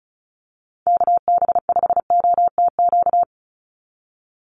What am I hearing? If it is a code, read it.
Morse code, heard as a keyed tone, about 34 wpm, 700 Hz. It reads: KB5OTQ
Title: USA Call Signs 35wpm